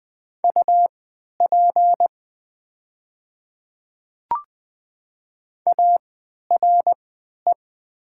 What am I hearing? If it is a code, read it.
Morse code, heard as a keyed tone, about 20 wpm, spaced as for 11 wpm, 700 Hz. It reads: UP ARE
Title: 3x Character Spacing - Top 100 Words - 20wpm